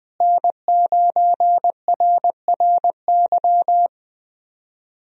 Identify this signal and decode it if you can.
Morse code, heard as a keyed tone, about 20 wpm, 700 Hz. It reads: N9RRY
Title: USA Call Signs - 5 Characters - Repeated 2x - 20wpm